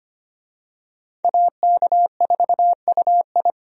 Morse code, heard as a keyed tone, about 25 wpm, 700 Hz. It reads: AK4UI
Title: USA Call Signs - 5 Characters - Repeated 3x - 25wpm